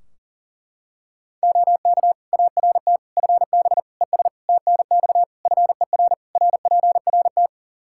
Morse code, heard as a keyed tone, about 40 wpm, 700 Hz. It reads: OK ART FB ES TNX FER RPRT
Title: Practice QSO - Exchange 2 - Info - 40wpm